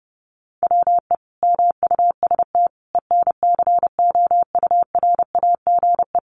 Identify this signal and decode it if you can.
Morse code, heard as a keyed tone, about 30 wpm, 700 Hz. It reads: WE MUST ENCOURAGE
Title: Sentences from Top 1400 Words 30wpm